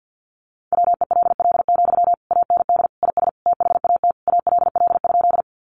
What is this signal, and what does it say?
Morse code, heard as a keyed tone, about 48 wpm, 700 Hz. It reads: WELL, AND IS THAT ALL?
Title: Sentences from Top 200 Words 50wpm